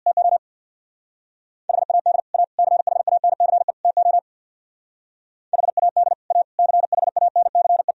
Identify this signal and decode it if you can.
Morse code, heard as a keyed tone, about 63 wpm, 700 Hz. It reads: TO HAD A CHANCE TO HAD A CHANCE